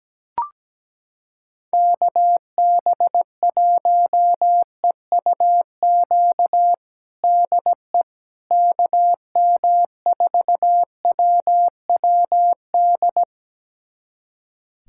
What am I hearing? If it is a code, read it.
Morse code, heard as a keyed tone, about 17 wpm, 700 Hz. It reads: KB1EUQ DE KM4WWD
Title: USA Callsign DE Callsign - 6 Characters - 17wpm